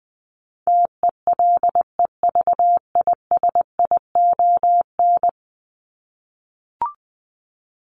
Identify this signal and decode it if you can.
Morse code, heard as a keyed tone, about 20 wpm, 700 Hz. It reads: TELEVISION
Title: Top 501-750 Words 20wpm